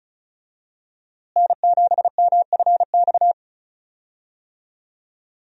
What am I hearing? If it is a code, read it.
Morse code, heard as a keyed tone, about 36 wpm, 700 Hz. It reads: N7MFX
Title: USA Call Signs - 5 Characters - Repeated 2x - 35wpm